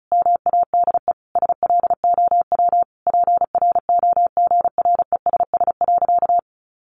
Morse code, heard as a keyed tone, about 34 wpm, 700 Hz. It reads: MADE SLOW PROGRESS.
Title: Sentences from Top 1400 Words 35wpm